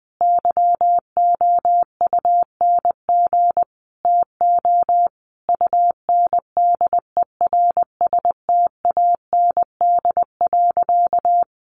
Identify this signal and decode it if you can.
Morse code, heard as a keyed tone, about 20 wpm, 700 Hz. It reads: YOUNG TO UNDERSTAND.
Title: Sentences from Top 600 Words 20wpm